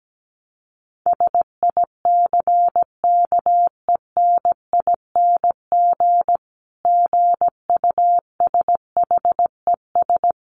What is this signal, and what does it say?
Morse code, heard as a keyed tone, about 17 wpm, 700 Hz. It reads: SICKENING GUSHES
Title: A Sound Of Thunder 17wpm